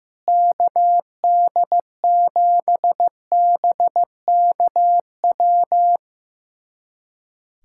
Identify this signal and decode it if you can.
Morse code, heard as a keyed tone, about 15 wpm, 700 Hz. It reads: KD7BKW
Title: USA Callsign DE Callsign - 6 Characters - 15wpm